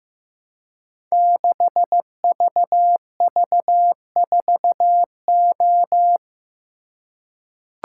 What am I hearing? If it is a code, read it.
Morse code, heard as a keyed tone, about 15 wpm, 700 Hz. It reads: 6VV4O
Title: Code Groups - 5 Letter:Numbers - 15wpm